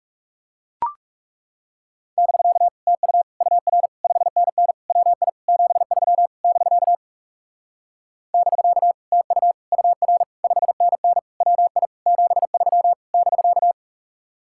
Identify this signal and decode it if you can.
Morse code, heard as a keyed tone, about 40 wpm, 700 Hz. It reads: <BK> TU UR 5NN WI 73 <BK> <BK> TU UR 5NN WI 73 <BK>